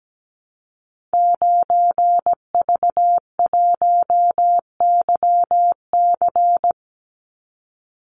Morse code, heard as a keyed tone, about 17 wpm, 700 Hz. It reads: 9V1YC
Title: Call Signs - 17wpm